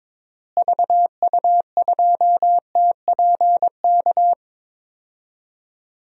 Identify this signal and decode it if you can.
Morse code, heard as a keyed tone, about 22 wpm, 700 Hz. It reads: VU2TPK
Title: DXCC-100 Call Signs - 6 Characters - Repeated 3x - 22wpm